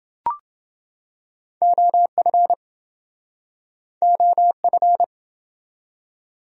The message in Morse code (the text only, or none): OF OF